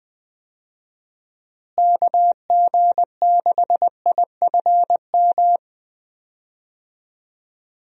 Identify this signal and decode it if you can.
Morse code, heard as a keyed tone, about 20 wpm, 700 Hz. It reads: KG6IFM
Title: USA Call Signs - 6 Characters - Repeated 2x - 20wpm